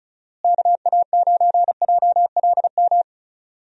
Morse code, heard as a keyed tone, about 36 wpm, 700 Hz. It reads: KA9JLM
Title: USA Call Signs - 6 Characters - Repeated 2x - 35wpm